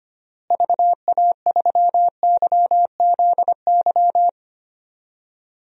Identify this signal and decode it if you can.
Morse code, heard as a keyed tone, about 25 wpm, 700 Hz. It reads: VA3YZY
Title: DXCC-100 Call Signs - 6 Characters - Repeated 2x - 25wpm